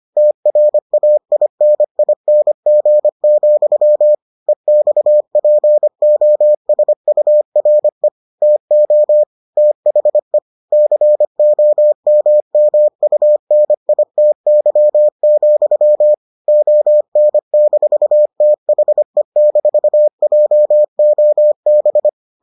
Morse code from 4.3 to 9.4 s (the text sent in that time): EXPOSURE TO